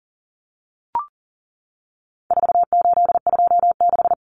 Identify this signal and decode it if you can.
Morse code, heard as a keyed tone, about 40 wpm, 700 Hz. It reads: 4826
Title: Numbers - 4-Digit 40wpm